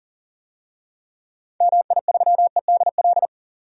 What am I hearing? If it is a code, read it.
Morse code, heard as a keyed tone, about 40 wpm, 700 Hz. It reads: MI3EDL
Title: DXCC-100 Call Signs - 6 Characters - 40wpm